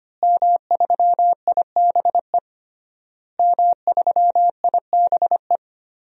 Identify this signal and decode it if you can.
Morse code, heard as a keyed tone, about 25 wpm, 700 Hz. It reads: M3IBE M3IBE
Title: DXCC-100 Call Signs - 5 Characters - Repeated 3x - 25wpm